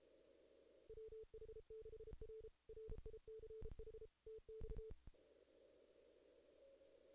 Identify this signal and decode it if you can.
Morse code, heard as a keyed tone, about 32 wpm, 440 Hz. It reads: WH6R RICH TX
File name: 2016 JAN CW NAQP / 2016 Jan naqp k5ot - 0-11